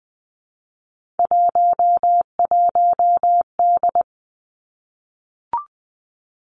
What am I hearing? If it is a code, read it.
Morse code, heard as a keyed tone, about 19 wpm, 700 Hz. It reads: 11D
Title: Three Challenging Letter-Number Combinations 20wpm